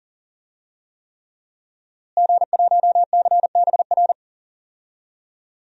Morse code, heard as a keyed tone, about 40 wpm, 700 Hz. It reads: G1CBR
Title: DXCC-100 Call Signs - 5 Characters - Repeated 3x - 40wpm